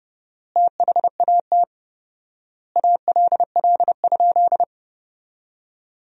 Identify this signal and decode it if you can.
Morse code, heard as a keyed tone, about 32 wpm, 700 Hz. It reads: THAT ALL?
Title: 4x Word-Spacing - Sentences from Top 100 Words - 30wpm